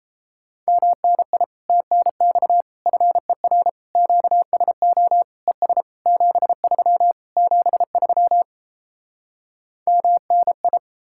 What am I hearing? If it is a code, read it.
Morse code, heard as a keyed tone, about 37 wpm, 700 Hz. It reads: MNI TNX FER QSO ES 73 73 MNI